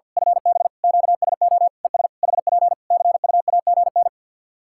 Morse code, heard as a keyed tone, about 51 wpm, 700 Hz. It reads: FB QSO ES HP CUAGN
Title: K1USN SST QSO Practice 50wpm